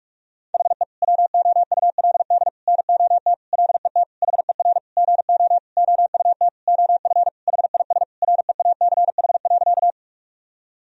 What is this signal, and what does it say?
Morse code, heard as a keyed tone, about 46 wpm, 700 Hz. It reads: HE WOULD NOT LET HER GO OUT OF HIS REACH.
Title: Sentences from Top 1000 Words - Review Part 2 - 45wpm